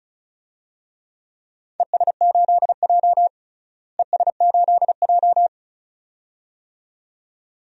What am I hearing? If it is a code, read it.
Morse code, heard as a keyed tone, about 34 wpm, 700 Hz. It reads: ES8J ES8J
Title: DXCC-100 Call Signs - 4 Characters - Repeated 2x - 35wpm